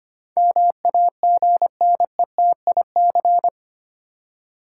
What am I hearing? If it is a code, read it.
Morse code, heard as a keyed tone, about 25 wpm, 690 Hz. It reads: MAGNETIC
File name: 8 Letter Words - First 3 Letters Given - 25wpm